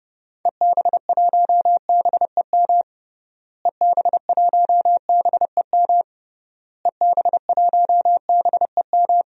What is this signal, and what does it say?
Morse code, heard as a keyed tone, about 30 wpm, 700 Hz. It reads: EB1BEM EB1BEM EB1BEM
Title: DXCC-100 Call Signs - 6 Characters - Repeated 3x - 30wpm